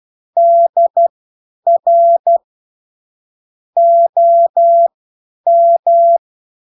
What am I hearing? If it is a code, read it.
Morse code, heard as a keyed tone, about 12 wpm, spaced as for 9 wpm, 680 Hz. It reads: DR OM